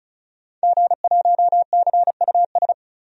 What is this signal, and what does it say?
Morse code, heard as a keyed tone, about 34 wpm, 700 Hz. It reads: G1CUS
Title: DXCC-100 Call Signs - 5 Characters - 35wpm